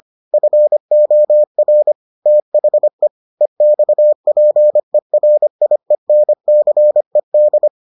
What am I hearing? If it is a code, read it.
Morse code, heard as a keyed tone, about 25 wpm, 600 Hz. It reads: FOR THE EXPERIENCED